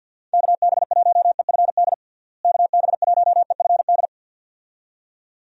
Morse code, heard as a keyed tone, about 52 wpm, 700 Hz. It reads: KB1EFD KB1EFD
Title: USA Call Signs - 6 Characters - Repeated 2x - 50wpm